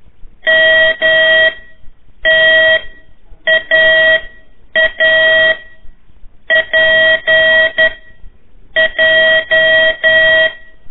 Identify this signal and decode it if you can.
Morse code, heard as a keyed tone, about 10 wpm, spaced as for 7 wpm, 1900 Hz, with noise. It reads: MTAAPJ
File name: 3 lezione CW 1 AIR